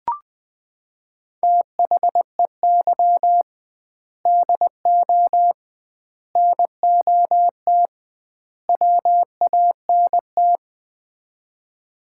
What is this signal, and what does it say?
Morse code, heard as a keyed tone, about 20 wpm, 700 Hz. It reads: THEY DO NOT WANT
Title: 2x Word-Spacing - Sets of 4 Words - 20wpm